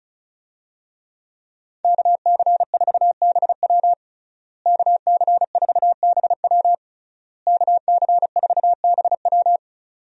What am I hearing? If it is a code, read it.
Morse code, heard as a keyed tone, about 36 wpm, 700 Hz. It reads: KC4BW KC4BW KC4BW
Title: USA Call Signs - 5 Characters - Repeated 3x - 35wpm